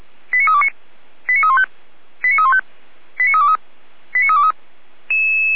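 Five-tone selective call (ZVEI), heard as quick runs of five tones, tones of about 70 ms. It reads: 89329 89315 89315 89323 89323
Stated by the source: ZVEI five-tone call 89502 received by SDR.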